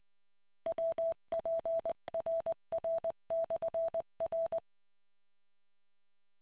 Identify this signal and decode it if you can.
Morse code, heard as a keyed tone, about 20 wpm, 660 Hz. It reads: WPRR/R